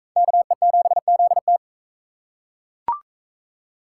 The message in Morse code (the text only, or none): KE7ZT